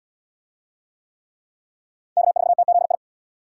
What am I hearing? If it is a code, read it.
Morse code, heard as a keyed tone, about 66 wpm, 700 Hz. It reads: K4TZI